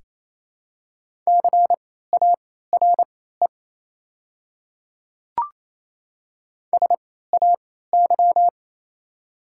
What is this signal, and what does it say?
Morse code, heard as a keyed tone, about 28 wpm, spaced as for 16 wpm, 700 Hz. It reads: CARE SAY
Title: 3x Character Spacing - Top 300 Words - 28wpm